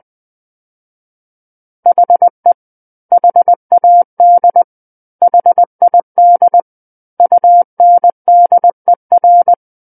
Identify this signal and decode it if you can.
Morse code, heard as a keyed tone, about 20 wpm, 700 Hz. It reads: HE HAD HID UNDER
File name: 20 WPM